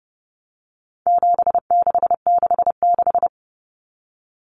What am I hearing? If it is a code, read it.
Morse code, heard as a keyed tone, about 30 wpm, 700 Hz. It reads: 7666
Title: Numbers - 4-Digit 30wpm